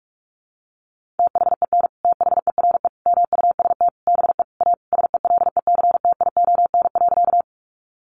Morse code, heard as a keyed tone, about 44 wpm, 700 Hz. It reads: THEN THERE MUST BE A SELECTION.
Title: Sentences from Top 1400 Words 45wpm